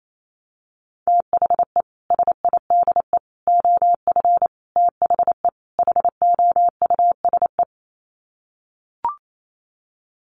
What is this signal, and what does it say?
Morse code, heard as a keyed tone, about 28 wpm, 700 Hz. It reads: THE SIDE OF THE HOUSE
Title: Sets of 5 Words Encore2 28wpm